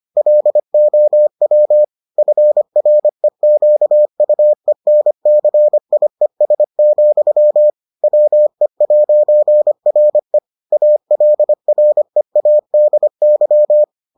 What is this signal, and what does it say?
Morse code, heard as a keyed tone, about 25 wpm, 600 Hz. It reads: LOW FREQUENCIES, WE'RE ALREADY